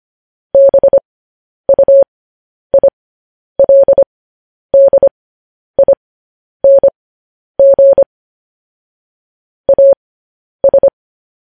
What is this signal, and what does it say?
Morse code, heard as a keyed tone, about 25 wpm, spaced as for 10 wpm, 550 Hz. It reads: BUILDING AS
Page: News Headlines in Morse Code at 10 WPM